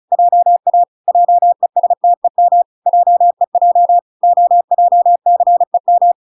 Morse code, heard as a keyed tone, about 34 wpm, 700 Hz. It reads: JA JESTEM JEJ OJCEM